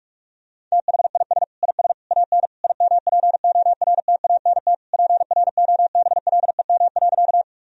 Morse code, heard as a keyed tone, about 47 wpm, 700 Hz. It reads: THIS IS AN IMPORTANT PROBLEM.
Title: Sentences from Top 1000 Words - Review Part 2 - 45wpm